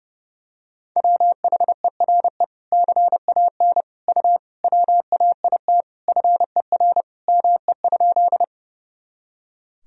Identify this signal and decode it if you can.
Morse code, heard as a keyed tone, about 30 wpm, 700 Hz. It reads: WHERE CAN U WAIT FER ME?
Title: Sentences from Top 500 Words and Abbreviations 30wpm